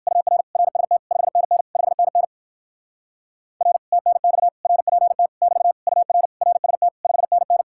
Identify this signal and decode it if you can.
Morse code, heard as a keyed tone, about 58 wpm, 700 Hz. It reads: UR RST 5NN 5NN R TNX RPT <BT> UR RST 5NN